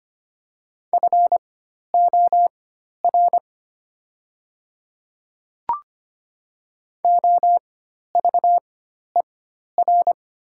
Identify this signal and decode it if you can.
Morse code, heard as a keyed tone, about 25 wpm, spaced as for 12 wpm, 690 Hz. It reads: FOR OVER
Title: 4x Character Spacing - Top 100 Words - 25wpm